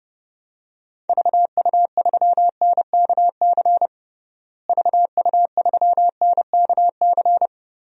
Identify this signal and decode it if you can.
Morse code, heard as a keyed tone, about 30 wpm, 700 Hz. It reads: VU3NKC VU3NKC
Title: DXCC-100 Call Signs - 6 Characters - Repeated 3x - 30wpm